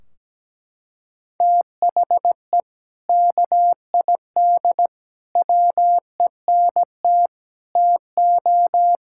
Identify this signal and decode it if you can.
Morse code, heard as a keyed tone, about 17 wpm, 700 Hz. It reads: THE KID WENT TO
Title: Sentences from Top 1600 Words 17wpm